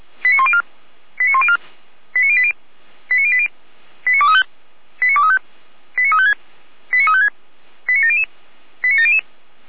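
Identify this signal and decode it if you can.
Five-tone selective call (ZVEI), heard as quick runs of five tones, tones of about 70 ms. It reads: 89104 89104 89080 89080 89235 89235 89356 89356 89700 89700
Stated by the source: ZVEI five-tone selective call received on SDR.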